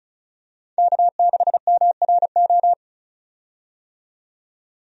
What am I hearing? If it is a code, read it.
Morse code, heard as a keyed tone, about 36 wpm, 700 Hz. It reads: K6MRO
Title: USA Call Signs - 5 Characters - 35wpm